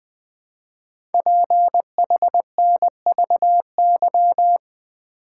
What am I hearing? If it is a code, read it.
Morse code, heard as a keyed tone, about 20 wpm, 700 Hz. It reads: PHNVY
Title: Code Groups - 5 Letters - 20wpm